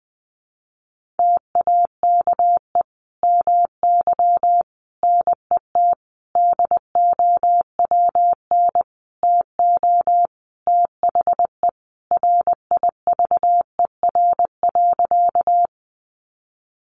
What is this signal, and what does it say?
Morse code, heard as a keyed tone, about 20 wpm, 700 Hz. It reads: TAKE MY NET DOWN TO THE RIVER.
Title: Sentences from Top 1400 Words 20wpm